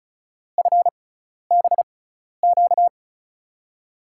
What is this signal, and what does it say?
Morse code, heard as a keyed tone, about 36 wpm, spaced as for 12 wpm, 690 Hz. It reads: FBQ
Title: USA Call Signs - 6 Characters - 6x Character Spacing - 35wpm